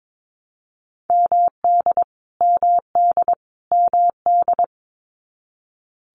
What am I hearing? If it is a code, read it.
Morse code, heard as a keyed tone, about 22 wpm, 700 Hz. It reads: MD MD MD
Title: US States Repeated 3x 22wpm